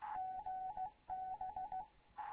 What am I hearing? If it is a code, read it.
Morse code, heard as a keyed tone, about 16 wpm, 750 Hz, with radio noise. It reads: GB
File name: Morze-HA2MN / hangp / 2 11 GYAKORLAT